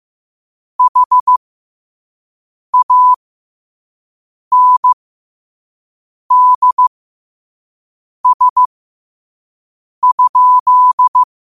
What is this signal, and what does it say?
Morse code, heard as a keyed tone, about 15 wpm, spaced as for 5 wpm, 1000 Hz. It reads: HANDS?
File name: qotdfiles / 5wpm